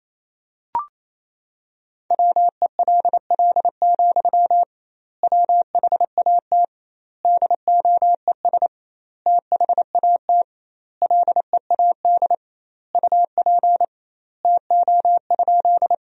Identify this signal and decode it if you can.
Morse code, heard as a keyed tone, about 28 wpm, 700 Hz. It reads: WELL, WHAT DOES THAT LEAD UP TO?
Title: 2x Word-Spacing - Sentences from Top 500 Words - 28wpm